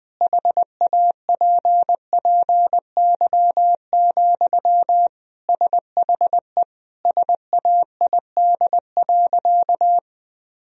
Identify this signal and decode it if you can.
Morse code, heard as a keyed tone, about 20 wpm, 700 Hz. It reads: HAPPY, SHE SAID.
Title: Sentences from Top 1000 Words - Review Part 1 - 20wpm